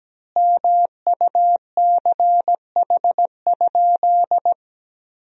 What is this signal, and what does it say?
Morse code, heard as a keyed tone, about 17 wpm, 700 Hz. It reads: MUCH?
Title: 4x Word-Spacing - Sentences from Top 200 Words - 17wpm